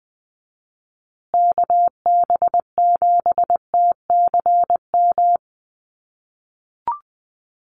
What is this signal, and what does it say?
Morse code, heard as a keyed tone, about 20 wpm, 700 Hz. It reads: KB7TCM
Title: USA Call Signs Repeated 2x 20wpm